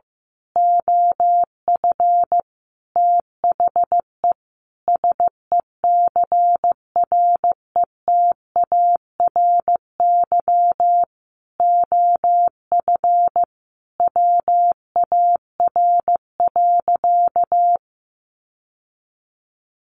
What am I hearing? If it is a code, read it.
Morse code, heard as a keyed tone, about 15 wpm, 700 Hz. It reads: OF THE SECRETARY OF WAR.
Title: Sentences from Top 1300 Words 15wpm